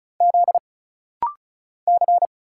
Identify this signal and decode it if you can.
Morse code, heard as a keyed tone, about 35 wpm, 700 Hz. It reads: Z C
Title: Single Letter-Number Alt - Rapid-Fire 35wpm